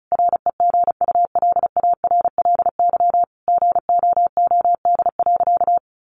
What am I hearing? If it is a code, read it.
Morse code, heard as a keyed tone, about 34 wpm, 700 Hz. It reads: REGULARLY GOOD.
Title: Sentences from Top 1400 Words 35wpm